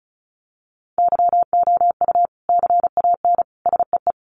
Morse code, heard as a keyed tone, about 34 wpm, 700 Hz. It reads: YOU CAN SEE
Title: Sets of 3 Words 35wpm